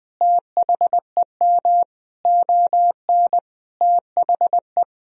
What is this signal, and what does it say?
Morse code, heard as a keyed tone, about 20 wpm, 700 Hz. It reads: THEM ON THE
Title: Sentences from Top 1000 Words - Review Part 1 - 20wpm